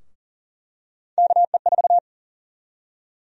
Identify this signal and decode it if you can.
Morse code, heard as a keyed tone, about 38 wpm, 700 Hz. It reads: KE4
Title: USA Call Sign Prefixes - 40wpm